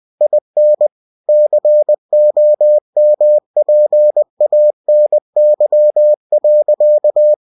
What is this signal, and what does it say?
Morse code, heard as a keyed tone, about 20 wpm, 600 Hz. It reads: IN COMPANY.
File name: cw-books / winnie-the-pooh / winnie-20wpm-page-0022